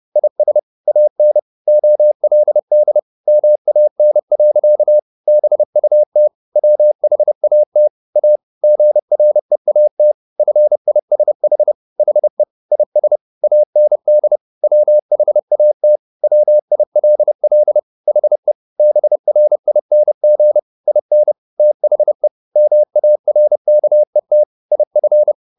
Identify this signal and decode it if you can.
Morse code, heard as a keyed tone, about 30 wpm, 600 Hz. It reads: IS AN OLD MAN. BUT WHAT A GREAT FISH HE IS AND WHAT WILL HE BRING IN THE MARKET IF
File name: the old man and the sea / the old man and the sea 1 30wpm